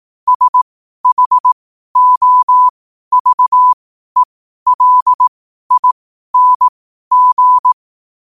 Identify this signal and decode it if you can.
Morse code, heard as a keyed tone, about 18 wpm, spaced as for 12 wpm, 1000 Hz. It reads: SHOVELING
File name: qotdfiles / 12wpm